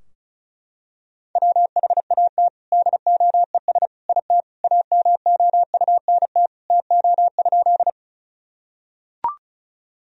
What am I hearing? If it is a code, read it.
Morse code, heard as a keyed tone, about 36 wpm, 700 Hz. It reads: WHAT DOES IT AMOUNT TO?
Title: Sentences from Top 1000 Words - Review Part 4 - 35wpm